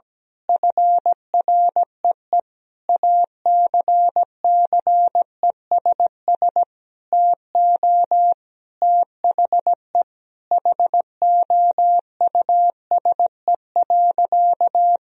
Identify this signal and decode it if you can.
Morse code, heard as a keyed tone, about 17 wpm, 700 Hz. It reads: FREE ACCESS TO THE HOUSE.
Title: Sentences from Top 1000 Words - Review Part 4 - 17wpm